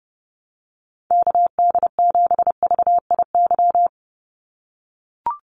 Morse code, heard as a keyed tone, about 30 wpm, 700 Hz. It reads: KD7VIY
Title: USA Call Signs 30wpm